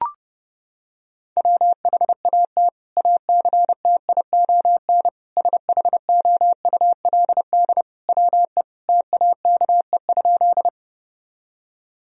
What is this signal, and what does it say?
Morse code, heard as a keyed tone, about 30 wpm, 700 Hz. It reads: WHAT ACTION SHOULD WE TAKE?
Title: Sentences from Top 1000 Words - Review Part 4 - 30wpm